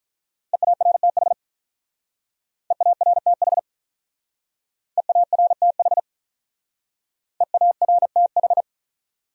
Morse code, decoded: EARTH EARTH EARTH EARTH

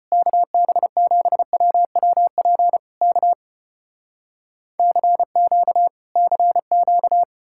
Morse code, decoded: KB7WWP K CQ CQ

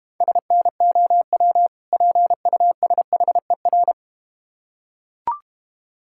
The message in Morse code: SNOW PUSHER

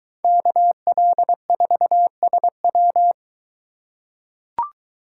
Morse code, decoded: KL4SW

